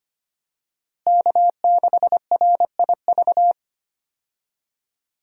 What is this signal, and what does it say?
Morse code, heard as a keyed tone, about 25 wpm, 700 Hz. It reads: K6RIV
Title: USA Call Signs - 5 Characters - Repeated 3x - 25wpm